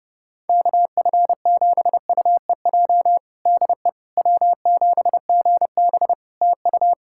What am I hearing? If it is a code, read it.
Morse code, heard as a keyed tone, about 30 wpm, 700 Hz. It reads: KF7UEJ DE W7GB TU
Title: Practice QSO - Exchange 4 - Ending - 30wpm